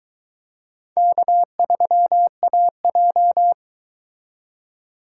Morse code, heard as a keyed tone, about 23 wpm, 700 Hz. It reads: K3AJ